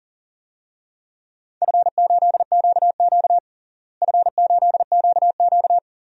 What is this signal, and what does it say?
Morse code, heard as a keyed tone, about 37 wpm, 700 Hz. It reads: F8QQ F8QQ
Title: DXCC-100 Call Signs - 4 Characters - Repeated 2x - 40wpm